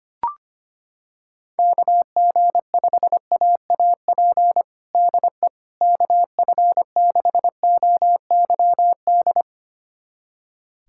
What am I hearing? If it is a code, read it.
Morse code, heard as a keyed tone, about 25 wpm, 700 Hz. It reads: KG5AAP DE KF6OYD
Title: USA Callsign DE Callsign - 6 Characters - 25wpm